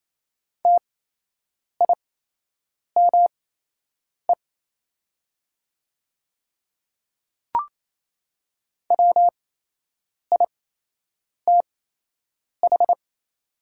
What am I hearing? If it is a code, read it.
Morse code, heard as a keyed tone, about 27 wpm, spaced as for 8 wpm, 700 Hz. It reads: TIME WITH
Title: 8x Character Spacing - Top 100 Words - 28wpm